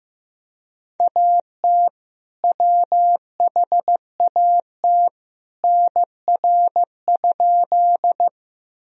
Morse code, heard as a keyed tone, about 15 wpm, 700 Hz. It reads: AT WHAT NR?
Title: Sentences from Top 100 Words and Abbreviations 15wpm